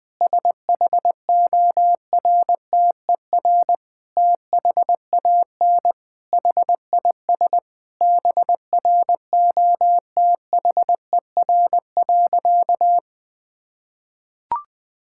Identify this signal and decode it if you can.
Morse code, heard as a keyed tone, about 20 wpm, 700 Hz. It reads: SHORTER THAN HIS BROTHER.
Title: Difficult Sentences with Hints - 20wpm